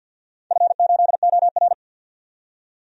Morse code, heard as a keyed tone, about 53 wpm, 700 Hz. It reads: F8OR
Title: DXCC-100 Call Signs - 4 Characters - 50wpm